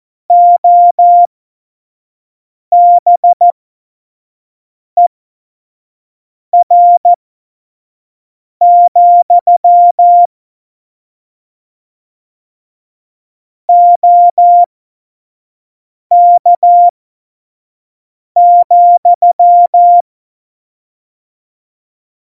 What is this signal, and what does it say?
Morse code, heard as a keyed tone, about 14 wpm, spaced as for 5 wpm, 700 Hz. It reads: OBER, OK,
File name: Morsekurs / Witze / Witz 26 Mai 2020, 25 F70